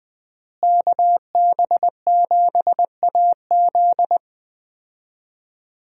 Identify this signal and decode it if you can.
Morse code, heard as a keyed tone, about 21 wpm, 700 Hz. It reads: KB7AZ
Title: USA Call Signs - 5 Characters - Repeated 3x - 20wpm